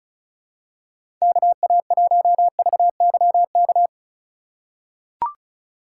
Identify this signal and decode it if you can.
Morse code, heard as a keyed tone, about 34 wpm, 700 Hz. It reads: KA1VYK
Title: USA Call Signs - 3x Character Spacing - 35wpm